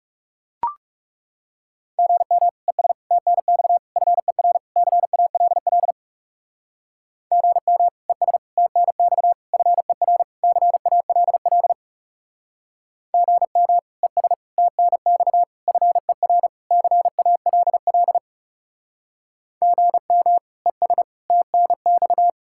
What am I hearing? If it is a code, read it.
Morse code, heard as a keyed tone, about 36 wpm, 700 Hz. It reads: GM ES TNX FER CALL GM ES TNX FER CALL GM ES TNX FER CALL GM ES TNX
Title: Speed-Racing Common QSO Phrases Encore - 30wpm